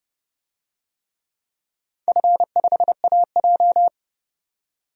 Morse code, heard as a keyed tone, about 30 wpm, 700 Hz. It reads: F5AJ
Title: DXCC-100 Call Signs - 4 Characters - 30wpm